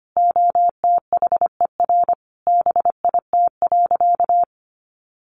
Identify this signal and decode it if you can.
Morse code, heard as a keyed tone, about 25 wpm, 700 Hz. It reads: OTHER BIT.
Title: Sentences from Top 900 Words 25wpm